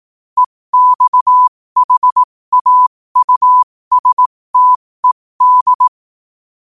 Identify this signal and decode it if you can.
Morse code, heard as a keyed tone, about 18 wpm, spaced as for 15 wpm, 1000 Hz. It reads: EXHAUSTED
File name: qotdfiles / 15wpm